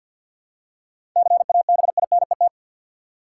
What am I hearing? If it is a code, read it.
Morse code, heard as a keyed tone, about 51 wpm, 700 Hz. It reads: CABINET
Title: Seven Letter Words - 50wpm